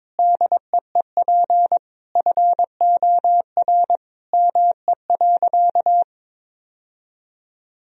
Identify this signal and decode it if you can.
Morse code, heard as a keyed tone, about 22 wpm, 700 Hz. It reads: DEEP FOR ME.
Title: Sentences from Top 1000 Words - Review Part 3 - 22wpm